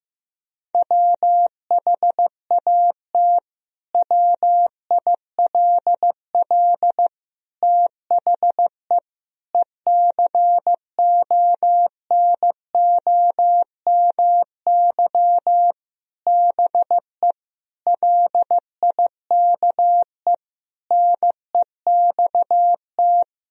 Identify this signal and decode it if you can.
Morse code, heard as a keyed tone, about 16 wpm, 700 Hz. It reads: WHAT WILL THE ECONOMY BE LIKE NEXT
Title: Sentences from Top 1000 Words - Review Part 3 - 15wpm